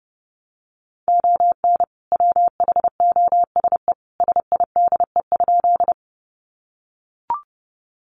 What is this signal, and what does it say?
Morse code, heard as a keyed tone, about 28 wpm, 700 Hz. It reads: ON WHOSE SIDE?
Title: Sentences from Top 1200 Words 30wpm